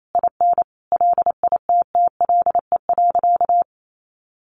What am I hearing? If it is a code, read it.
Morse code, heard as a keyed tone, about 28 wpm, 700 Hz. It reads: IN LITTLE.
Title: Sentences from Top 1500 Words 28wpm